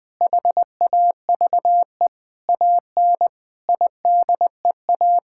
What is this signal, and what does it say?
Morse code, heard as a keyed tone, about 20 wpm, 700 Hz. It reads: HAVE AN IDEA
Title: Sentences with 9-Word Phrases - 20wpm